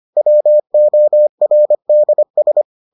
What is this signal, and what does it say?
Morse code, heard as a keyed tone, about 25 wpm, 600 Hz. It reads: WORDS